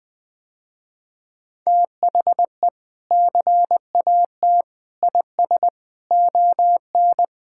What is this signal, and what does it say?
Morse code, heard as a keyed tone, about 21 wpm, 700 Hz. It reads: THE CAT IS ON
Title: Sentences with 3-Word Phrases - Spoken Beginning - 20wpm